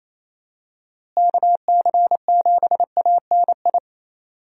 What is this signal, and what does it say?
Morse code, heard as a keyed tone, about 28 wpm, 700 Hz. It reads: KC7ANI
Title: USA Call Signs - 6 Characters - Repeated 3x - 28wpm